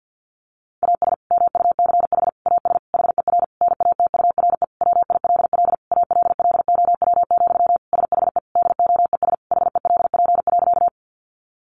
Morse code, heard as a keyed tone, about 50 wpm, 700 Hz. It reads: AS MUCH AS HER NATURE WILL ALLOW, SHE DOES HELP.
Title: Sentences from Top 700 Words 50wpm